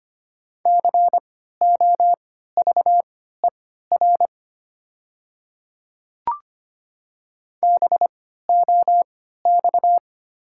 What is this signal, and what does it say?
Morse code, heard as a keyed tone, about 25 wpm, spaced as for 14 wpm, 690 Hz. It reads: COVER BOX
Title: 3x Character Spacing - Top 500 Words - 25wpm